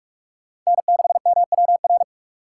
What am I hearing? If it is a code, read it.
Morse code, heard as a keyed tone, about 42 wpm, 700 Hz. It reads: N6MWR
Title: USA Call Signs - 5 Characters - 6x Character Spacing - 45wpm